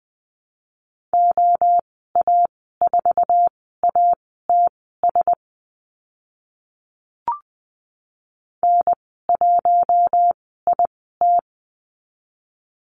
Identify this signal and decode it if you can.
Morse code, heard as a keyed tone, about 20 wpm, spaced as for 15 wpm, 700 Hz. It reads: OA4ATS N1IT
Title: 062 Lesson 11g - Letter h - Call Signs 20wpm